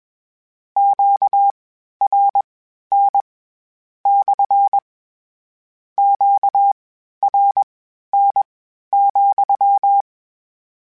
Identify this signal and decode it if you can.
Morse code, heard as a keyed tone, about 21 wpm, spaced as for 12 wpm, 800 Hz. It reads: QRN / QRN,
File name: Q Calls / Q Calls-mp3s / Q CALLS SPSL @20